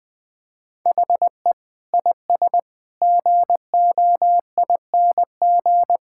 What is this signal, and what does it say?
Morse code, heard as a keyed tone, about 20 wpm, 700 Hz. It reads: HE IS GOING
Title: Sentences with 2-Word Phrases - 20wpm